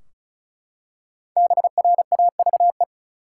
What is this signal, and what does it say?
Morse code, heard as a keyed tone, about 36 wpm, 700 Hz. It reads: BRAVE
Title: Five Letter Words - 35wpm